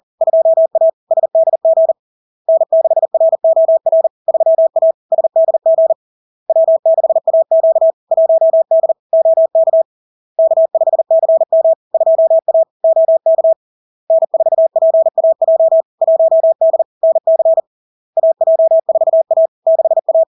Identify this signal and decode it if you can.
Morse code, heard as a keyed tone, about 40 wpm, 650 Hz. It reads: 2A SDG N6ROR 3A SDG W6AQ 1D OK K5CM 2A OK N4PAJ 1D NC AJ4A 6A
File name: LARK-FD-2022-Log-in-CW-40wpm-650-Hz